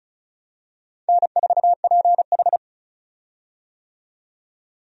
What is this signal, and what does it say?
Morse code, heard as a keyed tone, about 34 wpm, 700 Hz. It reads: N4PH
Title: USA Call Signs - 4 Characters - 35wpm